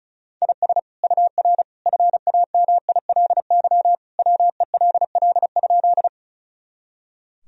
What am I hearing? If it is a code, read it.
Morse code, heard as a keyed tone, about 36 wpm, 700 Hz. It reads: IS UR FAMILY WELL?
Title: Sentences from Top 400 Words and Abbreviations 35wpm